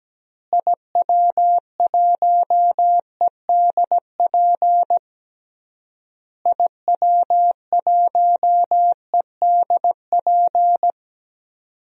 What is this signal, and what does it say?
Morse code, heard as a keyed tone, about 17 wpm, 700 Hz. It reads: IW1EDP IW1EDP
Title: DXCC-100 Call Signs - 6 Characters - Repeated 3x - 17wpm